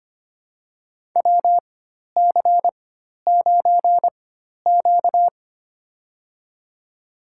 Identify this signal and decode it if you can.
Morse code, heard as a keyed tone, about 25 wpm, spaced as for 12 wpm, 700 Hz. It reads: WC9Q
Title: USA Call Signs - 4x Character Spacing - 25wpm